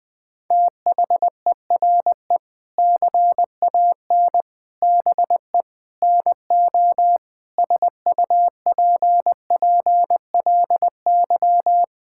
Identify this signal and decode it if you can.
Morse code, heard as a keyed tone, about 21 wpm, 700 Hz. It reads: THERE CAN BE NO SUPPLY
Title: Sentences from Top 1000 Words - Review Part 5 - 20wpm